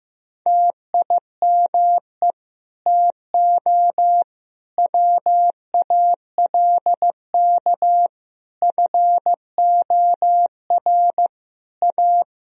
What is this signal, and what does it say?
Morse code, heard as a keyed tone, about 15 wpm, 700 Hz. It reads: TIME TO WALK FOR A
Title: Sentences from Top 1000 Words - Review Part 1 - 15wpm